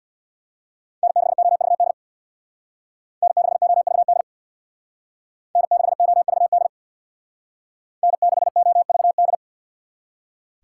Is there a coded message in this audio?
Yes: Morse code from 1.0 to 9.3 s, heard as a keyed tone, about 64 wpm, 700 Hz. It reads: N6OVD N6OVD N6OVD N6OVD